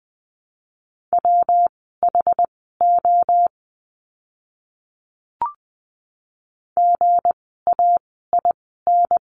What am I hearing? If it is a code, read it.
Morse code, heard as a keyed tone, about 19 wpm, spaced as for 14 wpm, 700 Hz. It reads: WHO GAIN
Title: 160 Lesson 26f - Letter g - Words 20wpm